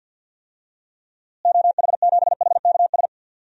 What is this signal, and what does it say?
Morse code, heard as a keyed tone, about 49 wpm, 700 Hz. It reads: OH7HKS
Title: DXCC-100 Call Signs - 6 Characters - Repeated 3x - 50wpm